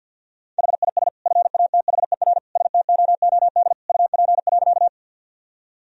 Morse code, heard as a keyed tone, about 51 wpm, 700 Hz. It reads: HIS FATHER STOOD UP.
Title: Sentences from Top 1000 Words - Review Part 2 - 50wpm